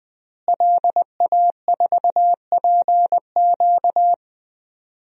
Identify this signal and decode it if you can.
Morse code, heard as a keyed tone, about 20 wpm, 700 Hz. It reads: LA4PQ
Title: DXCC-100 Call Signs - 5 Characters - Repeated 2x - 20wpm